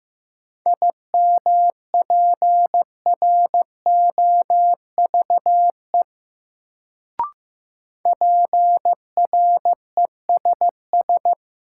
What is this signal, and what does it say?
Morse code, heard as a keyed tone, about 15 wpm, 700 Hz. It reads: IMPROVE PRESS
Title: Sentences from Top 800 Words - New Word Drill - 15wpm